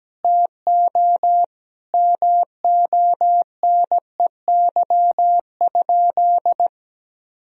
Morse code, heard as a keyed tone, about 17 wpm, 700 Hz. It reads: TO MONEY?
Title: Sentences from Top 1000 Words - Review Part 4 - 17wpm